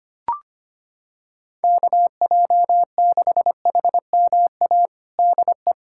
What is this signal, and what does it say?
Morse code, heard as a keyed tone, about 25 wpm, 700 Hz. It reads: KJ6HMA DE
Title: USA Callsign DE Callsign - 6 Characters - 25wpm